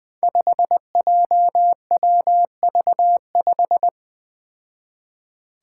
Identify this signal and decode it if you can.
Morse code, heard as a keyed tone, about 20 wpm, 700 Hz. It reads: 5JWV5
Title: Five Challenging Letter-Number Combinations 20wpm